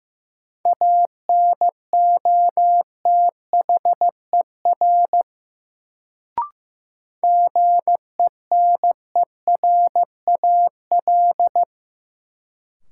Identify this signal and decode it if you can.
Morse code, heard as a keyed tone, about 16 wpm, 700 Hz. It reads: ANOTHER GENERAL
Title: Sentences from Top 600 Words - New Word Drill - 15wpm